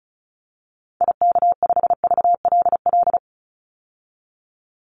Morse code, heard as a keyed tone, about 34 wpm, 700 Hz. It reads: IK5VLL
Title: Call Signs - 35wpm